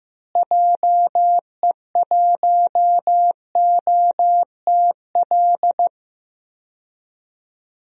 Morse code, heard as a keyed tone, about 15 wpm, 700 Hz. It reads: JE1OTL
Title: DXCC-100 Call Signs - 6 Characters - Repeated 2x - 15wpm